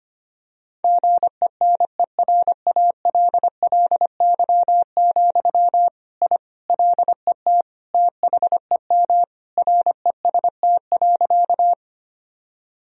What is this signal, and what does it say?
Morse code, heard as a keyed tone, about 25 wpm, 700 Hz. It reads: GENERALLY, I LET THEM REST.
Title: Sentences from Top 1000 Words - Review Part 4 - 25wpm